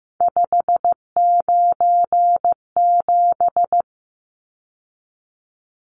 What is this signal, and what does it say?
Morse code, heard as a keyed tone, about 15 wpm, 700 Hz. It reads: 597
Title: Numbers - 3-Digit 15wpm